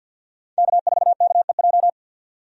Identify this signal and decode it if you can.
Morse code, heard as a keyed tone, about 52 wpm, 700 Hz. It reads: K4KEJ
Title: USA Call Signs - 5 Characters - 50wpm